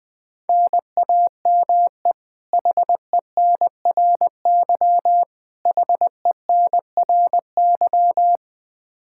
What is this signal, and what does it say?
Morse code, heard as a keyed tone, about 20 wpm, 700 Hz. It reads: NAME HENRY HENRY
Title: Practice QSO - Exchange 1 - Beginning - 20wpm